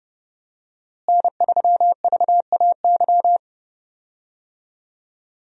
Morse code, heard as a keyed tone, about 30 wpm, 700 Hz. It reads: N3VAY
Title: USA Call Signs - 5 Characters - Repeated 2x - 30wpm